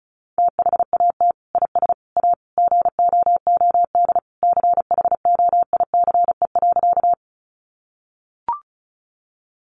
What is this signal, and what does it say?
Morse code, heard as a keyed tone, about 34 wpm, 700 Hz. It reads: THAT IS A GOOD CHOICE.
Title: Sentences from Top 900 Words 35wpm